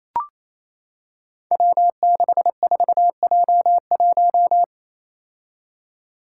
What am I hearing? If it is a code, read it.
Morse code, heard as a keyed tone, about 28 wpm, 700 Hz. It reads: W64J1
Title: Five Challenging Letter-Number Combinations 28wpm